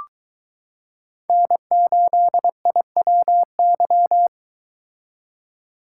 Morse code, heard as a keyed tone, about 20 wpm, 700 Hz. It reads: N8IWY